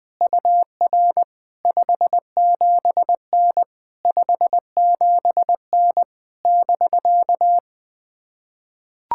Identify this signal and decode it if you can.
Morse code, heard as a keyed tone, about 20 wpm, 700 Hz. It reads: UR 57N 57N <BK>
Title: POTA QSO Practice - 20wpm